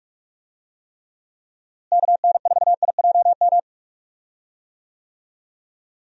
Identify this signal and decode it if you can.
Morse code, heard as a keyed tone, about 46 wpm, 700 Hz. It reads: KN4IJM
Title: USA Call Signs - 6 Characters - Repeated 2x - 45wpm